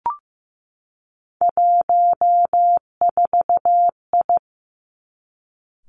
Morse code, heard as a keyed tone, about 15 wpm, 700 Hz. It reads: 14I
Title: Three Challenging Letter-Number Combinations 15wpm